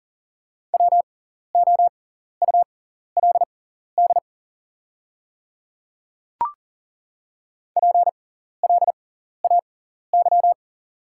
Morse code, heard as a keyed tone, about 39 wpm, spaced as for 14 wpm, 690 Hz. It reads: WOULD PLAY
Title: 6x Character Spacing - Top 200 Words - 40wpm